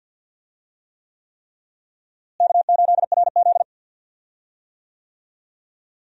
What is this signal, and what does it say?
Morse code, heard as a keyed tone, about 47 wpm, 700 Hz. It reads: K8RZ
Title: USA Call Signs - 4 Characters - Repeated 3x - 50wpm